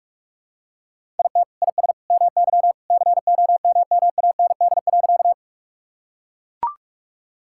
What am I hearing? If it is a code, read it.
Morse code, heard as a keyed tone, about 44 wpm, 700 Hz. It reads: IT IS MY COMMAND.
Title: Sentences from Top 1600 Words 45wpm